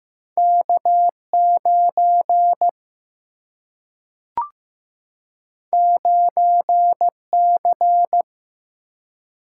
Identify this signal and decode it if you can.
Morse code, heard as a keyed tone, about 15 wpm, 700 Hz. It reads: K9 9C
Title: Single-Letter Single-Number 15wpm